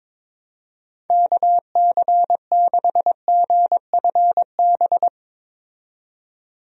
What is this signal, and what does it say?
Morse code, heard as a keyed tone, about 22 wpm, 700 Hz. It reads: KC6GFB
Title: USA Call Signs - 6 Characters - 22wpm